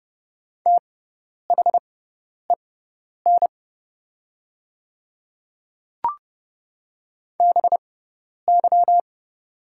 Morse code, heard as a keyed tone, about 30 wpm, spaced as for 10 wpm, 700 Hz. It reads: THEN BY
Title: 6x Character Spacing - Top 100 Words - 30wpm